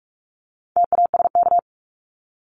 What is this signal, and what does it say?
Morse code, heard as a keyed tone, about 44 wpm, 700 Hz. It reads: TASK
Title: Top 751-1000 Words 45wpm